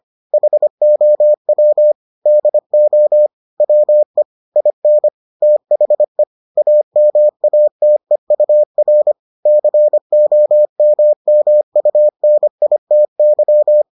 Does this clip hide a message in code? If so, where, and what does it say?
Yes: Morse code from 0.3 to 13.8 s, heard as a keyed tone, about 25 wpm, 600 Hz. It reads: HOW DO WE IN THE AMATEUR COMMUNITY